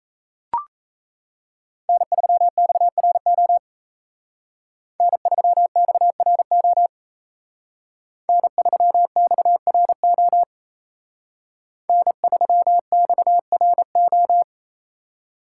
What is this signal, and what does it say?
Morse code, heard as a keyed tone, about 32 wpm, 700 Hz. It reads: N3XRO N3XRO N3XRO N3XRO
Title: Speed Racing - USA 1x3 Call Signs 28wpm